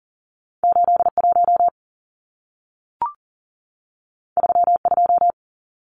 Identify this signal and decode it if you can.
Morse code, heard as a keyed tone, about 40 wpm, 700 Hz. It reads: 81 32
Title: Numbers - 2-Digit 40wpm